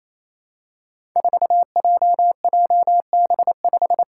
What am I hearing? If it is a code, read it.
Morse code, heard as a keyed tone, about 28 wpm, 700 Hz. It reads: 4JJB5
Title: Code Groups - 5 Letter:Numbers - 28wpm